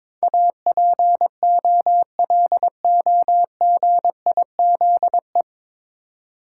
Morse code, heard as a keyed tone, about 22 wpm, 700 Hz. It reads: APOLOGIZE
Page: Morse Code Ninja practice 22wpm